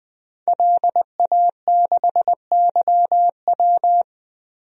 Morse code, heard as a keyed tone, about 21 wpm, 700 Hz. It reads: LA6YW
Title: DXCC-100 Call Signs - 5 Characters - 20wpm